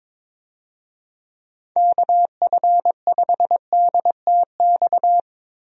Morse code, heard as a keyed tone, about 22 wpm, 700 Hz. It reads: KF5DTX
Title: USA Call Signs - 6 Characters - Repeated 3x - 22wpm